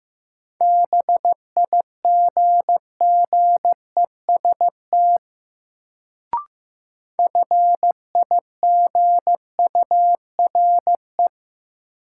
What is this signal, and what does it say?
Morse code, heard as a keyed tone, about 15 wpm, 700 Hz. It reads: BIGGEST FIGURE
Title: Sets of 4 Words - New Word Drill - 15wpm